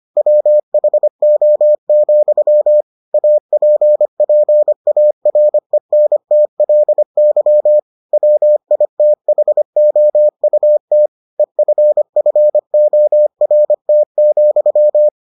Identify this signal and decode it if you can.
Morse code, heard as a keyed tone, about 25 wpm, 600 Hz. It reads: WHO, APPARENTLY WITHOUT EFFORT,